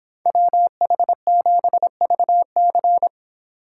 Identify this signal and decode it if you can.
Morse code, heard as a keyed tone, about 29 wpm, 700 Hz. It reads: WH7VC